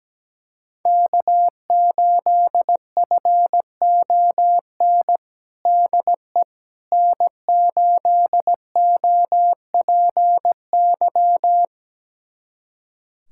Morse code, heard as a keyed tone, about 17 wpm, 700 Hz. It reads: K8FON DE N8OPY
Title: USA Callsign DE Callsign - 5 Characters - 17wpm